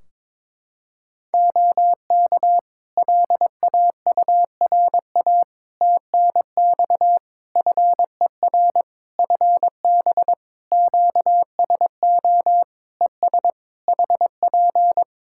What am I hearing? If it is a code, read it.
Morse code, heard as a keyed tone, about 22 wpm, 700 Hz. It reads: OK LAURA TNX FER FB QSO ES HP
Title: K1USN SST QSO Practice 22wpm